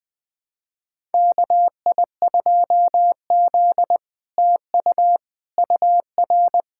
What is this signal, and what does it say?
Morse code, heard as a keyed tone, about 21 wpm, 700 Hz. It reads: KI2Z TU UR
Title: SOTA QSO Practice 20wpm